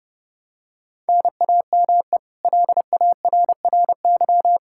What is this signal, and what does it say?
Morse code, heard as a keyed tone, about 30 wpm, 700 Hz. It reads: NAME LARRY
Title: Practice QSO - Exchange 1 - Beginning - 30wpm